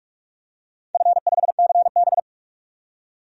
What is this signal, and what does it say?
Morse code, heard as a keyed tone, about 46 wpm, 700 Hz. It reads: F5CB
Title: DXCC-100 Call Signs - 4 Characters - 45wpm